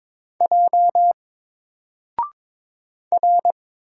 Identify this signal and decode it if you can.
Morse code, heard as a keyed tone, about 22 wpm, 700 Hz. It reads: J R
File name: Single-Letter-Number-22wpm